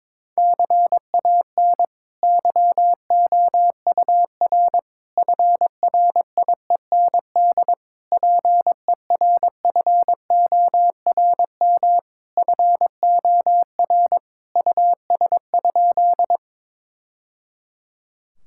Morse code, heard as a keyed tone, about 22 wpm, 700 Hz. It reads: CAN YOUR FRIEND PERFORM FOR US?
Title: Sentences from Top 1000 Words - Review Part 5 - 22wpm